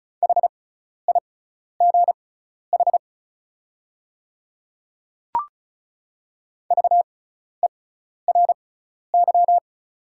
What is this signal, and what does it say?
Morse code, heard as a keyed tone, about 34 wpm, spaced as for 12 wpm, 700 Hz. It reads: HIGH VERY
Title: 6x Character Spacing - Top 300 Words - 35wpm